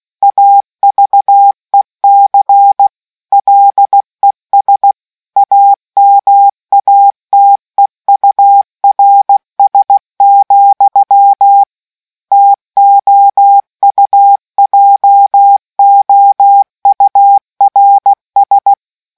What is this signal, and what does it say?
Morse code, heard as a keyed tone, about 16 wpm, 800 Hz. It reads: AVEC LES AMATEURS, TOUJOURS